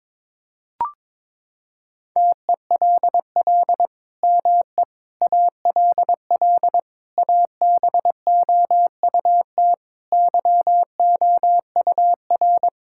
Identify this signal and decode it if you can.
Morse code, heard as a keyed tone, about 22 wpm, 700 Hz. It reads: TELL ME ALL ABOUT YOUR
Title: Sentences from Top 1000 Words - Review Part 5 - 22wpm